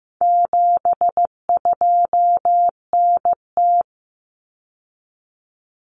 Morse code, heard as a keyed tone, about 15 wpm, 700 Hz. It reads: 72NT
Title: Common Cut Numbers - 4-Digit 15wpm